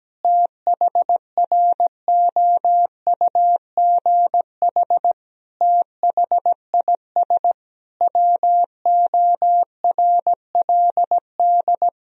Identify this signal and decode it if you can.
Morse code, heard as a keyed tone, about 17 wpm, 700 Hz. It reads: THROUGH THIS WORLD
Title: Sentences from Top 1000 Words - Review Part 2 - 17wpm